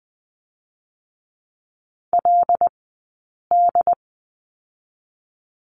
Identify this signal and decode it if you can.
Morse code, heard as a keyed tone, about 21 wpm, 700 Hz. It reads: L D
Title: 137 Lesson 23d - Letter f - Two Characters 20wpm